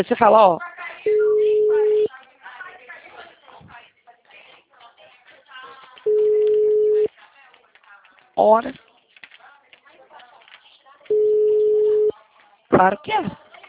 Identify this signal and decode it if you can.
Ringback tone (Europe, 425 Hz), with speech.
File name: out- -2022- - - .61424